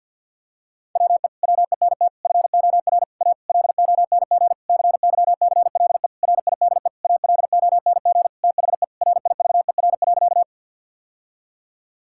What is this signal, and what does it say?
Morse code, heard as a keyed tone, about 50 wpm, 700 Hz. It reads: WE WENT FOR A LONG CYCLE RIDE ALONG THE RIVER.
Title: Sentences from Top 1000 Words - Review Part 6 - 50wpm